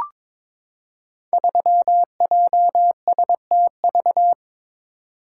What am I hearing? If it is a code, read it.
Morse code, heard as a keyed tone, about 22 wpm, 700 Hz. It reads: 3JSTV
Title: Code Groups - 5 Letter:Numbers - 22wpm